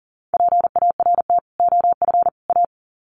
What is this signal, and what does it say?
Morse code, heard as a keyed tone, about 40 wpm, 700 Hz. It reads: PART OF A
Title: Sets of 3 Words 40wpm